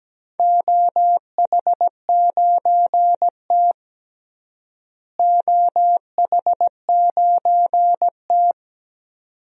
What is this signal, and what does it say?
Morse code, heard as a keyed tone, about 17 wpm, 700 Hz. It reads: OH9T OH9T
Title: DXCC-100 Call Signs - 4 Characters - Repeated 2x - 17wpm